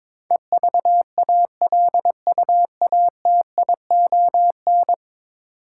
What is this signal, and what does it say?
Morse code, heard as a keyed tone, about 22 wpm, 700 Hz. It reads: EVALUATION
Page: morse-code-ninja-practice-22wpm